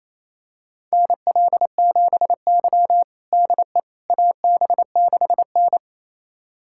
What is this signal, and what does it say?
Morse code, heard as a keyed tone, about 28 wpm, 700 Hz. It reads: NL7Y DE AB6N
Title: USA Callsign DE Callsign - 4 Characters - 28wpm